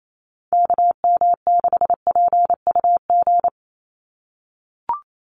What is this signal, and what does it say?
Morse code, heard as a keyed tone, about 28 wpm, 700 Hz. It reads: KM6PUG
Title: USA Call Signs 28wpm